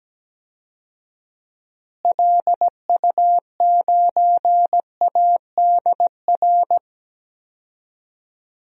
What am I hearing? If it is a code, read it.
Morse code, heard as a keyed tone, about 17 wpm, 700 Hz. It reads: LU9ADR
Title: DXCC-100 Call Signs - 6 Characters - Repeated 2x - 17wpm